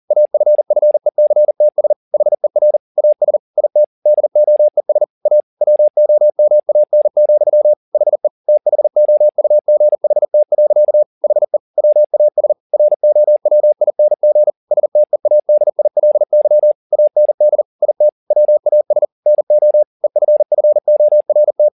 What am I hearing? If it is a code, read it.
Morse code, heard as a keyed tone, about 40 wpm, 600 Hz. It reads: AFFECTS HER AS IT DOES A WOMAN, HE THOUGHT. HE WAS ROWING STEADILY AND IT WAS NO EFFORT